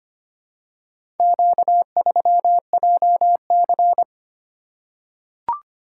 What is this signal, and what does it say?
Morse code, heard as a keyed tone, about 25 wpm, 700 Hz. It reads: Q3JC
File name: practice / 25wpm / Four Letter-Number Combinations - 25wpm